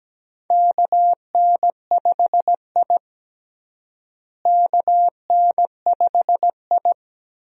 Morse code, heard as a keyed tone, about 17 wpm, 700 Hz. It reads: KN5I KN5I
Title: USA Call Signs - 4 Characters - Repeated 2x - 17wpm